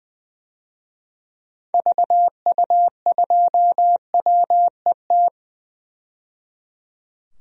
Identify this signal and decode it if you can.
Morse code, heard as a keyed tone, about 20 wpm, 700 Hz. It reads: VU2WET
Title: DXCC-100 Call Signs - 6 Characters - 20wpm